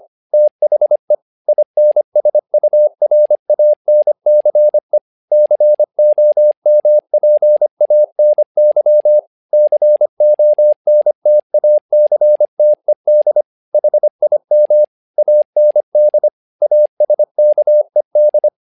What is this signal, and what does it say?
Morse code, heard as a keyed tone, about 25 wpm, 600 Hz. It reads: THE INSURANCE COMPANY CONTACTED HIM AND ASKED
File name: learning center / tower trouble / tower trouble 25wpm